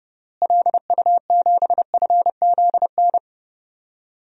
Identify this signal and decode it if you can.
Morse code, heard as a keyed tone, about 30 wpm, 700 Hz. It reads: LU7FZN
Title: DXCC-100 Call Signs - 6 Characters - Repeated 2x - 30wpm